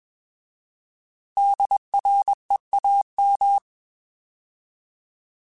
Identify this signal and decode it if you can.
Morse code, heard as a keyed tone, about 20 wpm, 800 Hz. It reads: DREAM
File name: Alphabetical / Alphabetical mp3s / Ds @20